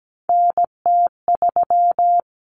Morse code, heard as a keyed tone, about 17 wpm, 700 Hz. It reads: NT3
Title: Common Cut Numbers - 3-Digit 17wpm